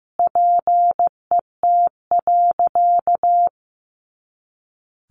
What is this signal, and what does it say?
Morse code, heard as a keyed tone, about 15 wpm, 700 Hz. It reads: PET.
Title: Easy Phrases 15wpm